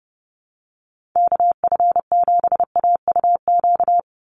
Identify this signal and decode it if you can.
Morse code, heard as a keyed tone, about 30 wpm, 700 Hz. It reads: KF7AUQ
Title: USA Call Signs 30wpm